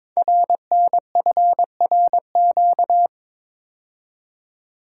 Morse code, heard as a keyed tone, about 22 wpm, 700 Hz. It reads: RNFRQ
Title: Code Groups - 5 Letters - 22wpm